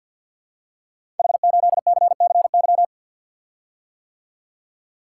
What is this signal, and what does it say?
Morse code, heard as a keyed tone, about 53 wpm, 700 Hz. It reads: H8CCY
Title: Code Groups - 5 Letter:Numbers - 50wpm